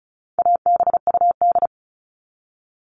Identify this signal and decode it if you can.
Morse code, heard as a keyed tone, about 34 wpm, 700 Hz. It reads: ABUD
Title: All Cut Numbers - 4-Digit 35wpm